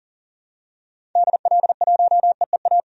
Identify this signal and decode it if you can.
Morse code, heard as a keyed tone, about 40 wpm, 700 Hz. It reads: DL1EEA
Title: DXCC-100 Call Signs - 6 Characters - 40wpm